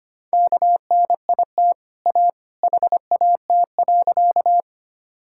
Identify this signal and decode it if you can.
Morse code, heard as a keyed tone, about 25 wpm, 700 Hz. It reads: KNIT A HAT.
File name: Easy Sentences with Hints - 25wpm